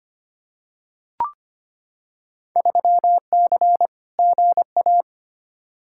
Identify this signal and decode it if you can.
Morse code, heard as a keyed tone, about 25 wpm, 700 Hz. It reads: 3C GA
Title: Field Day Exchanges Encore - 25wpm